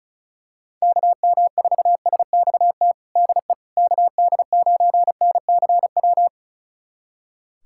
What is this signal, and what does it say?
Morse code, heard as a keyed tone, about 36 wpm, 700 Hz. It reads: KM4SXT DE KD9NCW
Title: USA Callsign DE Callsign - 6 Characters - 35wpm